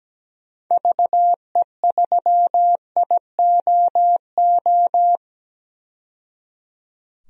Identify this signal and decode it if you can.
Morse code, heard as a keyed tone, about 17 wpm, 700 Hz. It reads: VE3IOO
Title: DXCC-100 Call Signs - 6 Characters - 17wpm